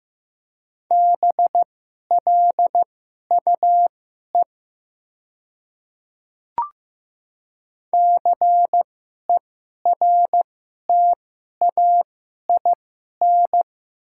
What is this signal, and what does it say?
Morse code, heard as a keyed tone, about 15 wpm, spaced as for 11 wpm, 700 Hz. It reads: BLUE CERTAIN
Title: 2x Character Spacing - Top 500 Words - 15wpm